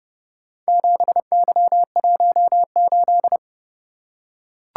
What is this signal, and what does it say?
Morse code, heard as a keyed tone, about 29 wpm, 700 Hz. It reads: 7Y18